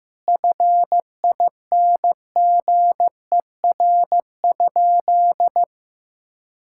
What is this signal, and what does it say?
Morse code, heard as a keyed tone, about 15 wpm, 700 Hz. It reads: FINGER?
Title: Sentences from Top 1600 Words 15wpm